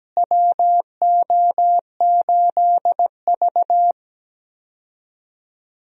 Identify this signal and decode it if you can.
Morse code, heard as a keyed tone, about 17 wpm, 700 Hz. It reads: WO8V
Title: USA Call Signs - 4 Characters - Repeated 2x - 17wpm